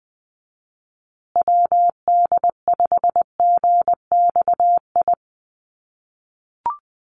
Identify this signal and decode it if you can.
Morse code, heard as a keyed tone, about 20 wpm, 700 Hz. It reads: WD5GXI
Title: USA Call Signs Repeated 2x 20wpm